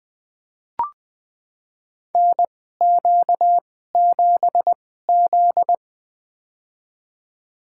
Morse code, heard as a keyed tone, about 20 wpm, spaced as for 14 wpm, 700 Hz. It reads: NQ7Z
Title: USA Call Signs - 2x Character Spacing - 20wpm